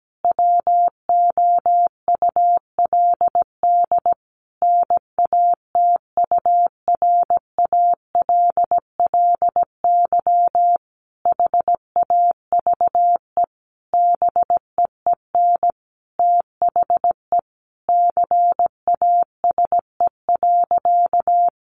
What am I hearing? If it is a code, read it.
Morse code, heard as a keyed tone, about 17 wpm, 700 Hz. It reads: WOULD NATURALLY HAVE BEEN THE CASE.
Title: Sentences from Top 1300 Words 17wpm